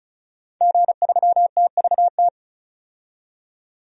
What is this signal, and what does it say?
Morse code, heard as a keyed tone, about 36 wpm, 700 Hz. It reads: G3TVT
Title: DXCC-100 Call Signs - 5 Characters - 35wpm